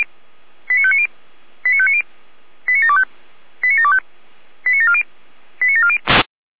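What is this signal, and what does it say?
Five-tone selective call (ZVEI), heard as quick runs of five tones, tones of about 70 ms. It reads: 89590 89590 89725 89725 89740 89740
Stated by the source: ZVEI five-tone selective call 89735, received on an SDR.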